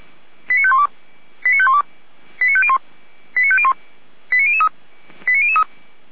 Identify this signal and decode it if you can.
Five-tone selective call (ZVEI), heard as quick runs of five tones, tones of about 70 ms. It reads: 89512 89512 89501 89501 89003 89003